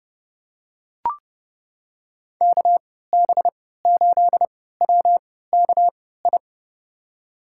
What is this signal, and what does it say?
Morse code, heard as a keyed tone, about 30 wpm, spaced as for 17 wpm, 700 Hz. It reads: KB8WKI
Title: USA Call Signs - 6 Characters - 3x Character Spacing - 30wpm